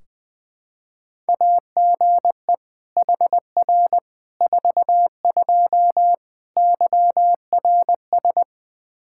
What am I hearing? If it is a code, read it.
Morse code, heard as a keyed tone, about 20 wpm, 700 Hz. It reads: AGE HR 42 YRS
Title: Practice QSO - Exchange 3 - More Info - 20wpm